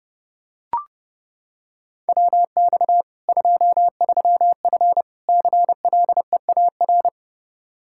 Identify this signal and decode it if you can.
Morse code, heard as a keyed tone, about 32 wpm, 700 Hz. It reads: WX 23F CLEAR
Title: K1USN SST QSO Practice 30wpm